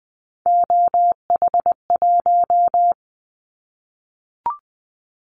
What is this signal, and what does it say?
Morse code, heard as a keyed tone, about 20 wpm, 700 Hz. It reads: OH1
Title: Callsign Prefixes for Top 50 DXCC - 20wpm